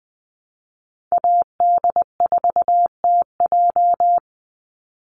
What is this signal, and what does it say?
Morse code, heard as a keyed tone, about 20 wpm, 700 Hz. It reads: AD4TJ
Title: Call Signs Repeated 3x - 20wpm